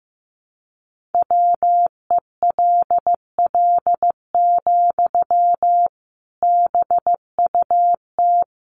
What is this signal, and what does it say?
Morse code, heard as a keyed tone, about 15 wpm, 700 Hz. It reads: WELL, BUT
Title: Sentences from Top 700 Words 15wpm